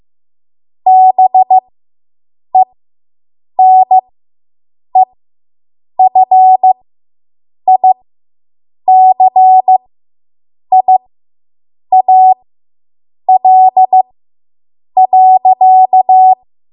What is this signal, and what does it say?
Morse code, heard as a keyed tone, about 15 wpm, spaced as for 7 wpm, 750 Hz. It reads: BENEFICIAL.